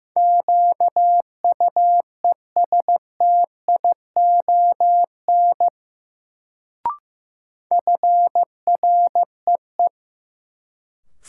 Morse code, heard as a keyed tone, about 15 wpm, 700 Hz. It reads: QUESTION FREE
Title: Sets of 5 Words Encore - New Word Drill - 15wpm